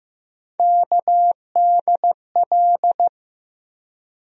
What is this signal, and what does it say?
Morse code, heard as a keyed tone, about 15 wpm, 700 Hz. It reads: KDL
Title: USA Call Sign Postfixes - 3 Letters 15wpm